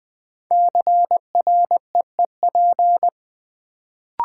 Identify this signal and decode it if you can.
Morse code, heard as a keyed tone, about 20 wpm, 700 Hz. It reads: CREEP
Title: Build A Word - 1 Letter At A Time - 5-Letter Words - 20wpm